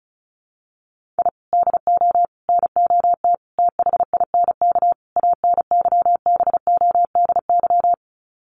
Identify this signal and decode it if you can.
Morse code, heard as a keyed tone, about 34 wpm, 700 Hz. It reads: I DO NOT THINK ANYBODY
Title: Sets of 5 Words 35wpm